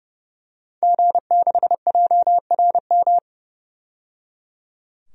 Morse code, heard as a keyed tone, about 29 wpm, 700 Hz. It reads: G6JRM
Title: DXCC-100 Call Signs - 5 Characters - 30wpm